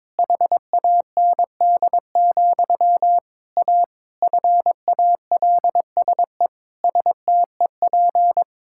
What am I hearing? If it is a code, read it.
Morse code, heard as a keyed tone, about 22 wpm, 700 Hz. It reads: HAND, A FALSE STEP
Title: Sentences from Top 1600 Words 22wpm